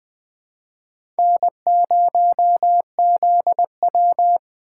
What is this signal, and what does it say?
Morse code, heard as a keyed tone, about 21 wpm, 700 Hz. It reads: N0ZW
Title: Speed Racing - USA 1x2 Call Signs 17wpm